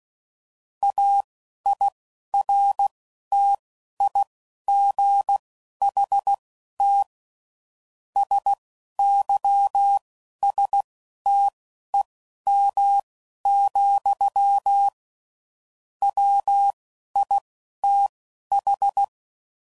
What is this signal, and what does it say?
Morse code, heard as a keyed tone, about 16 wpm, spaced as for 11 wpm, 800 Hz. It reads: AIRTIGHT SYSTEM, WITH
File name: Sayings / Quotes / Airtight